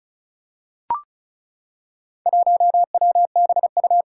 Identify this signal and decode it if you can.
Morse code, heard as a keyed tone, about 36 wpm, 700 Hz. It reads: 1WBU
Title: Four Challenging Letter-Number Combinations 35wpm